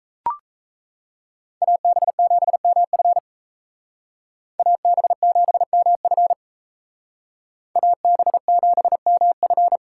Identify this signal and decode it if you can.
Morse code, heard as a keyed tone, about 36 wpm, 700 Hz. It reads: AB7MF AB7MF AB7MF